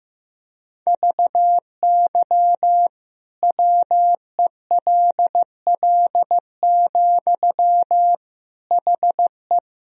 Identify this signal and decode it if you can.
Morse code, heard as a keyed tone, about 15 wpm, 700 Hz. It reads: VY WELL, HE
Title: Sentences from Top 200 Words and Abbreviations 15wpm